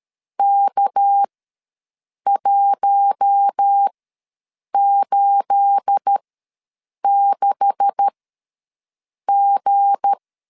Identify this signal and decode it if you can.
Morse code, heard as a keyed tone, about 13 wpm, spaced as for 7 wpm, 800 Hz. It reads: K186G
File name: LEZIONI / 9 286 ws 12wpm 3351